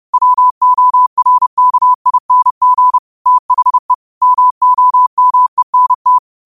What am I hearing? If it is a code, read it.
Morse code, heard as a keyed tone, about 30 wpm, 1000 Hz. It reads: WORKING THE MOMENT